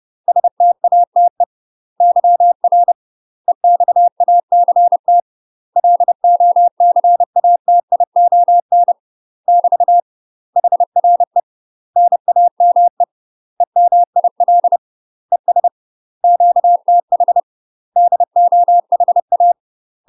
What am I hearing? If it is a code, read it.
Morse code, heard as a keyed tone, about 30 wpm, 700 Hz. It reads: STATE YR EXACT LOCATION <BT> HRE NAME EMIL ES QTH DOHA
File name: UK MASTER QSOs TRAINING / 30WPM / UK MASTER QSOs 0023